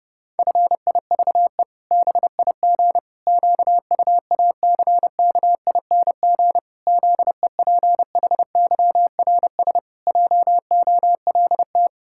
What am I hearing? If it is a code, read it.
Morse code, heard as a keyed tone, about 30 wpm, 700 Hz. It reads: FIVE BIG QUACKING ZEPHYRS JOLT
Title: Pangram Challenge v2 - 30wpm